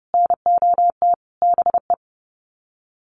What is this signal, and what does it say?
Morse code, heard as a keyed tone, about 30 wpm, 700 Hz. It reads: NOT BE
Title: Sets of 5 Words 30wpm